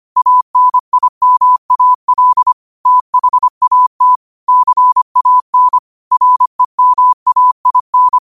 Morse code, heard as a keyed tone, about 25 wpm, 1000 Hz. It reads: ANIMAL THAT CAN REMAIN
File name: qotdfiles / 25wpm